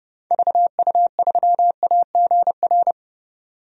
Morse code, heard as a keyed tone, about 30 wpm, 700 Hz. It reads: VU3AGR
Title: DXCC-100 Call Signs - 6 Characters - 30wpm